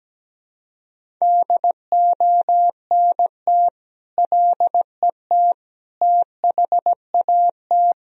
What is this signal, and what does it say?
Morse code, heard as a keyed tone, about 17 wpm, 700 Hz. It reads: DONT LET THAT
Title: Sentences from Top 1000 Words - Review Part 1 - 17wpm